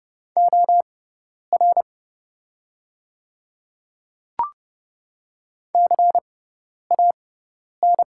Morse code, heard as a keyed tone, about 30 wpm, spaced as for 10 wpm, 700 Hz. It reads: OR CAN
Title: 6x Character Spacing - Top 100 Words - 30wpm